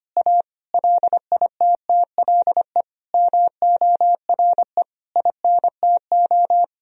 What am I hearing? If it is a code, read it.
Morse code, heard as a keyed tone, about 25 wpm, 700 Hz. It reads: A LITTLE MORE INTO
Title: Sentences from Top 1000 Words - Review Part 2 - 25wpm